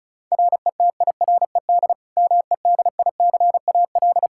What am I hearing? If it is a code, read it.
Morse code, heard as a keyed tone, about 36 wpm, 700 Hz. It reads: RETIRED MEDICAL
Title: Practice QSO - Exchange 3 - More Info - 35wpm